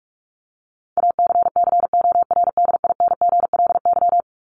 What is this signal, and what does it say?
Morse code, heard as a keyed tone, about 44 wpm, 700 Hz. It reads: ACCORDINGLY
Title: LY Suffix - Top 100 Words with Base Words 45wpm